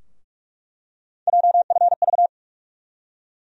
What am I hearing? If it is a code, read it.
Morse code, heard as a keyed tone, about 41 wpm, 700 Hz. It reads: JFV
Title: USA Call Sign Postfixes - 3 Letters 45wpm